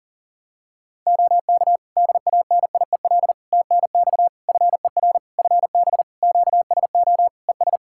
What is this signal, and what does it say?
Morse code, heard as a keyed tone, about 40 wpm, 700 Hz. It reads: OK DANIEL TNX FER FB QSO ES
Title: Practice QSO - Exchange 4 - Ending - 40wpm